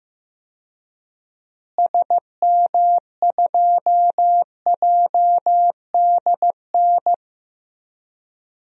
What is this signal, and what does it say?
Morse code, heard as a keyed tone, about 15 wpm, 700 Hz. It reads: SM2JDN
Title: DXCC-100 Call Signs - 6 Characters - Repeated 2x - 15wpm